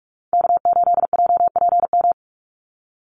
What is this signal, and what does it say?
Morse code, heard as a keyed tone, about 44 wpm, 700 Hz. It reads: K8JPM
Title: Call Signs Repeated 2x - 45wpm